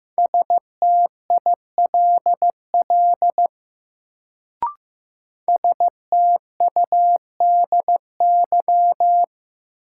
Morse code, heard as a keyed tone, about 16 wpm, 700 Hz. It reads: STILL STUDY
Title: Sentences from Top 300 Words - New Word Drill - 15wpm